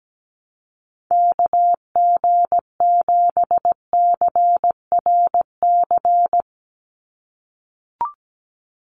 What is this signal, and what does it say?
Morse code, heard as a keyed tone, about 17 wpm, 700 Hz. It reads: KG7CRC
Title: USA Call Signs 17wpm